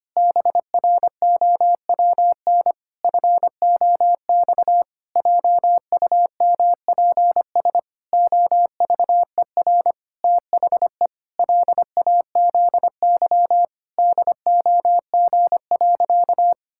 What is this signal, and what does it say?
Morse code, heard as a keyed tone, about 25 wpm, 700 Hz. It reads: BROWN FOX JUMPS OVER THE LAZY DOG.
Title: Pangram Challenge v2 - 25wpm